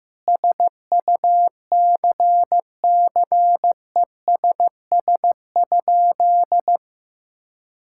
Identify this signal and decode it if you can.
Morse code, heard as a keyed tone, about 15 wpm, 700 Hz. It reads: SUCCESS?
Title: Sentences from Top 1000 Words - Review Part 5 - 15wpm